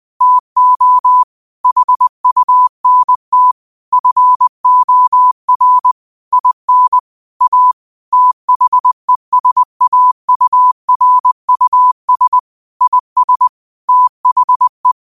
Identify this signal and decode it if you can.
Morse code, heard as a keyed tone, about 20 wpm, 1000 Hz. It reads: TO HUNT FOR IN A THESAURUS IS THE
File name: qotdfiles / 20wpm